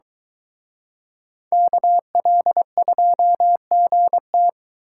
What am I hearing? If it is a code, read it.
Morse code, heard as a keyed tone, about 22 wpm, 700 Hz. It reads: KL2GT